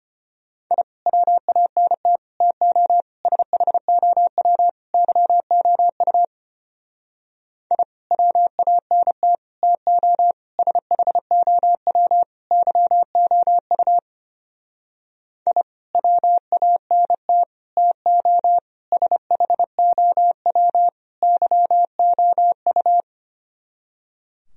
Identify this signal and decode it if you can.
Morse code, heard as a keyed tone, about 31 wpm, 700 Hz. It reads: I WANT TO SHOW YOU I WANT TO SHOW YOU I WANT TO SHOW YOU
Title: Speed-Racing - Sets of 5 Words 25wpm